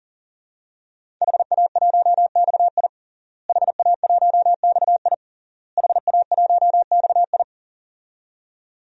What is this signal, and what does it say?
Morse code, heard as a keyed tone, about 39 wpm, 700 Hz. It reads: HA1XI HA1XI HA1XI
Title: DXCC-100 Call Signs - 5 Characters - Repeated 3x - 40wpm